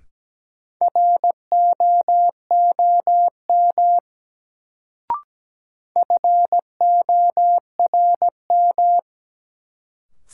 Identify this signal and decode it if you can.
Morse code, heard as a keyed tone, about 17 wpm, 700 Hz. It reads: ROOM FORM
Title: Sets of 4 Words - New Word Drill - 17wpm